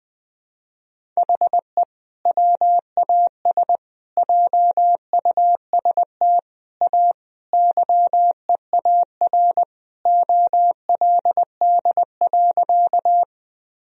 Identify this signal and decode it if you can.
Morse code, heard as a keyed tone, about 21 wpm, 700 Hz. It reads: HE WAS JUST A YEAR OLD.
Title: Sentences from Top 1000 Words - Review Part 1 - 20wpm